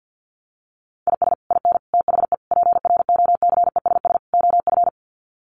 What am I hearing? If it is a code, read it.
Morse code, heard as a keyed tone, about 50 wpm, 700 Hz. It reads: IS IN THE PROCESS OF
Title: Sets of 5 Words Encore2 50wpm